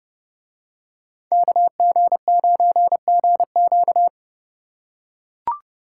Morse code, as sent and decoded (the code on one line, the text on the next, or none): -.- --. ----. --. --.-
KG9GQ